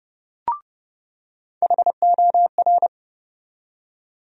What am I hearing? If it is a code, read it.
Morse code, heard as a keyed tone, about 30 wpm, 700 Hz. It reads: HOR
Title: USA Call Sign Postfixes - 3 Letters 30wpm